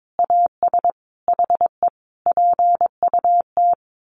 Morse code, heard as a keyed tone, about 22 wpm, 700 Hz. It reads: AS HE PUT
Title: A Sound Of Thunder 22wpm